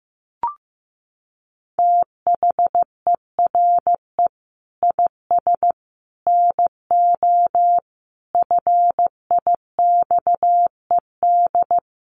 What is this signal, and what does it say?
Morse code, heard as a keyed tone, about 15 wpm, 700 Hz. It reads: THERE IS NO FIXED
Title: Sentences from Top 1100 Words 15wpm